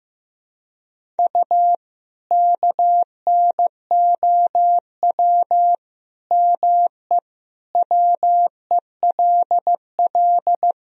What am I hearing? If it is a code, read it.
Morse code, heard as a keyed tone, about 15 wpm, 700 Hz. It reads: U KNOW ME WELL
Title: Sentences from Top 400 Words and Abbreviations 15wpm